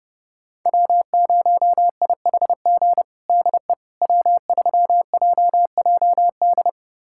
Morse code, heard as a keyed tone, about 30 wpm, 700 Hz. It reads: W0IHG DE W3JJD
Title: USA Callsign DE Callsign - 5 Characters - 30wpm